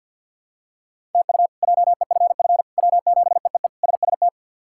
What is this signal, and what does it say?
Morse code, heard as a keyed tone, about 46 wpm, 700 Hz. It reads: TU JEFF W7EEE SST
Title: K1USN SST QSO Practice - 50wpm